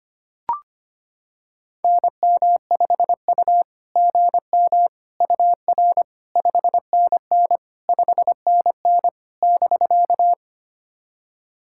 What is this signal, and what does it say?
Morse code, heard as a keyed tone, about 25 wpm, 690 Hz. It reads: NM5U GM UR 5NN 5NN <BK>
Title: SOTA QSO Practice 25wpm